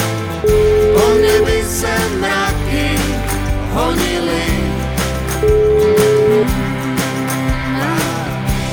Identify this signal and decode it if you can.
Ringback tone (Europe, 425 Hz), with noise.